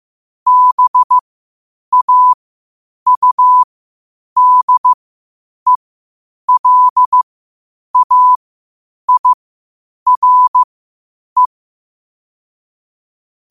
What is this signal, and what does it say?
Morse code, heard as a keyed tone, about 15 wpm, spaced as for 8 wpm, 1000 Hz. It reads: BAUDELAIRE
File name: qotdfiles / 8wpm